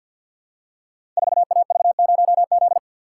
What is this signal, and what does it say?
Morse code, heard as a keyed tone, about 52 wpm, 690 Hz. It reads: 4AV0Z
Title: Code Groups - 5 Letter:Numbers - 50wpm